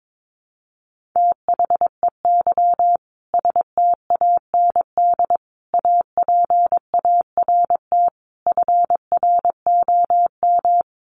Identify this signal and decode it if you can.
Morse code, heard as a keyed tone, about 22 wpm, 700 Hz. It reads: THEY STAND APART FROM
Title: Easy Phrases 22wpm